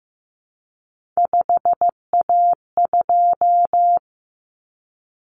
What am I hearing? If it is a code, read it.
Morse code, heard as a keyed tone, about 15 wpm, 700 Hz. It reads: 5A2
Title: Common Cut Numbers - 3-Digit 15wpm